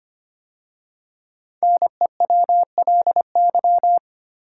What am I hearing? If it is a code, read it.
Morse code, heard as a keyed tone, about 25 wpm, 700 Hz. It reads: NEWLY
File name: 5 Letter Words - First 3 Letters Given - 25wpm